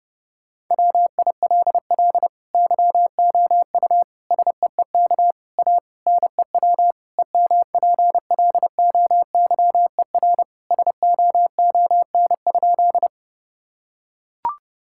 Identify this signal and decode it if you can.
Morse code, heard as a keyed tone, about 30 wpm, 700 Hz. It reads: WILL YOU SEEK A NEW EMPLOYER SOON?
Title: Sentences from Top 1600 Words 30wpm